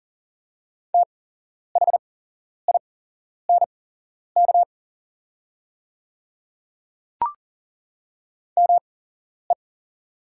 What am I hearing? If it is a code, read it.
Morse code, heard as a keyed tone, about 41 wpm, spaced as for 11 wpm, 700 Hz. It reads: THINK ME
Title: 8x Character Spacing - Top 300 Words - 40wpm